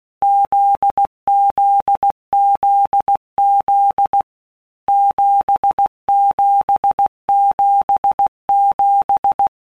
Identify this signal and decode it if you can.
Morse code, heard as a keyed tone, about 16 wpm, 800 Hz. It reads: ZZZZ 7777
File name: Basic Letter Groups By Blocks / Groups@15 / JZ17@15